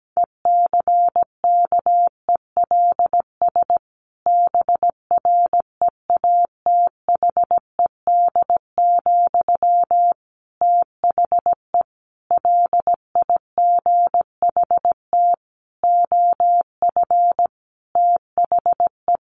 Morse code, heard as a keyed tone, about 17 wpm, 700 Hz. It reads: ECKELS BREATHED, THE LIGHT OF THE
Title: A Sound Of Thunder 17wpm